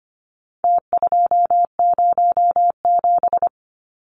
Morse code, heard as a keyed tone, about 25 wpm, 700 Hz. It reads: T207
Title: Numbers - Contest Style - 4-Digit 25wpm